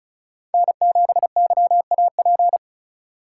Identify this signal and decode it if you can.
Morse code, heard as a keyed tone, about 36 wpm, 700 Hz. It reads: N7YAP
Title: USA Call Signs - 5 Characters - 4x Character Spacing - 35wpm